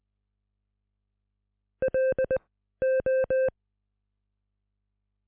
Morse code, heard as a keyed tone, about 20 wpm, spaced as for 13 wpm, 520 Hz. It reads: LO